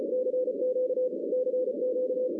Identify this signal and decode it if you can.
Morse code, heard as a keyed tone, about 34 wpm, 500 Hz, with noise. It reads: CQ CQ
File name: 2016 arrl dx cw / 2016-ARRL-DX-CW-K5TR- - 0-25